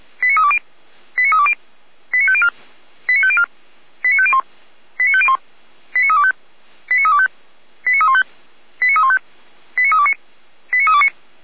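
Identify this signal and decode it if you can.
Five-tone selective call (ZVEI), heard as quick runs of five tones, tones of about 80 ms. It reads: 89320 89320 89504 89504 89501 89501 89325 89325 89315 89315 89329 89329
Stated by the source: ZVEI five-tone call 89376 received on an SDR.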